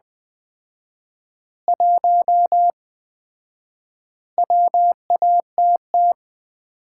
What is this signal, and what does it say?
Morse code, heard as a keyed tone, about 20 wpm, 700 Hz. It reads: 1 WATT
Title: Common QSO Phrases - 4x Word-Spacing - 20wpm